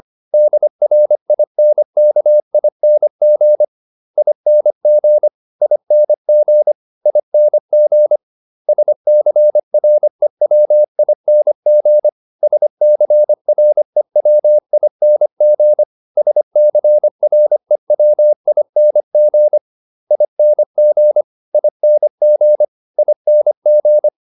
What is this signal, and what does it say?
Morse code, heard as a keyed tone, about 25 wpm, 600 Hz. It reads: DRINKING ING ING ING SCREWING SCREWING SCREWING ING ING ING